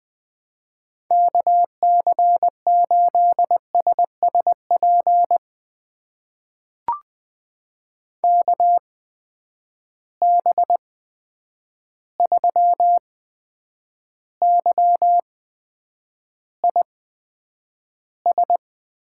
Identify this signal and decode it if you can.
Morse code, heard as a keyed tone, about 20 wpm, spaced as for 9 wpm, 700 Hz. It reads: KC8SSP KB3YIS
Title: USA Call Signs - 8x Character Spacing - 20wpm